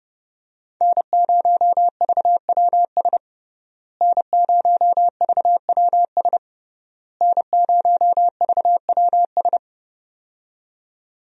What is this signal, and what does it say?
Morse code, heard as a keyed tone, about 30 wpm, 700 Hz. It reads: N0VWS N0VWS N0VWS
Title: USA Call Signs - 5 Characters - Repeated 3x - 30wpm